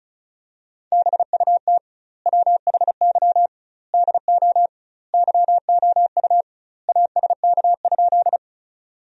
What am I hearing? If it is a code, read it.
Morse code, heard as a keyed tone, about 36 wpm, 700 Hz. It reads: BUT WHY DO YOU ASK?
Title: 2x Word-Spacing - Sentences from Top 200 Words - 35wpm